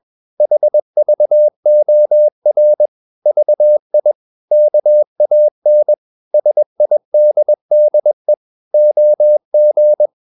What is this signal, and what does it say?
Morse code, heard as a keyed tone, about 21 wpm, 600 Hz. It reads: HVOR VI KAN SIDDE OG
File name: Ferie QRS 21wpm